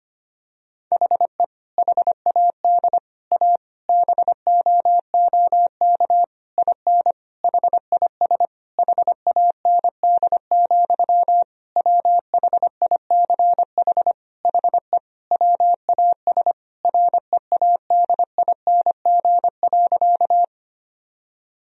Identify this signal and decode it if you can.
Morse code, heard as a keyed tone, about 25 wpm, 700 Hz. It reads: HE HAD A BOOK IN HIS HAND, WHICH HE WAS READING.
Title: Sentences from Top 1000 Words - Review Part 3 - 25wpm